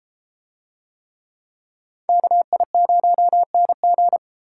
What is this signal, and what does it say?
Morse code, heard as a keyed tone, about 33 wpm, 700 Hz. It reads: KI0NG